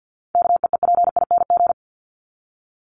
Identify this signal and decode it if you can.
Morse code, heard as a keyed tone, about 50 wpm, 700 Hz. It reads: KEEPING
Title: ING Suffix - Top 100 Words with Base Words 50wpm